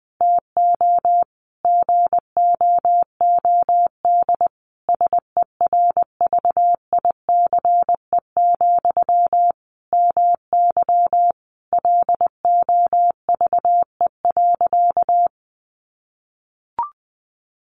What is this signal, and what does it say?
Morse code, heard as a keyed tone, about 20 wpm, 700 Hz. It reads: TO GOOD SERVICE, MY LOVE.
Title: Sentences from Top 1100 Words 20wpm